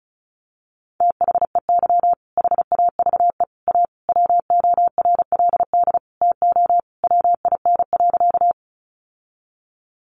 Morse code, heard as a keyed tone, about 34 wpm, 700 Hz. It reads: THEY HAVE A WORLD TO WIN.
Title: Sentences from Top 1500 Words 35wpm